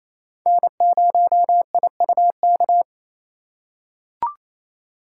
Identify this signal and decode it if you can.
Morse code, heard as a keyed tone, about 28 wpm, 700 Hz. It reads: N0IUK
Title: USA Call Signs - 6x Character Spacing - 28wpm